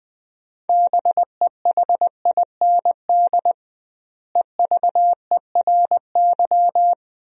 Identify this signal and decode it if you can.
Morse code, heard as a keyed tone, about 20 wpm, 700 Hz. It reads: BEHIND EVERY
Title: 2x Word-Spacing - Sentences from Top 500 Words - 20wpm